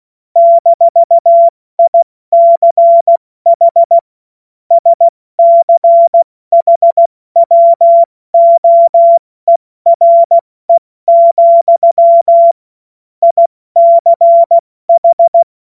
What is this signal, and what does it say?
Morse code, heard as a keyed tone, about 16 wpm, 680 Hz. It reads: -ICH SCHWOERE, ICH
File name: Morsekurs / Witze / Witz 31. Mai 2022 70 F80